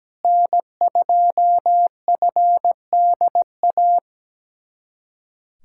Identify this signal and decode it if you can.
Morse code, heard as a keyed tone, about 17 wpm, 700 Hz. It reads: N2FDA
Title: USA Callsign DE Callsign 17wpm